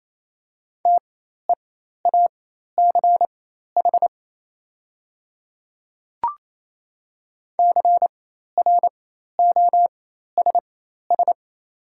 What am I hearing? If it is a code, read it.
Morse code, heard as a keyed tone, about 28 wpm, spaced as for 13 wpm, 700 Hz. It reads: TEACH CROSS
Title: 4x Character Spacing - Top 500 Words - 28wpm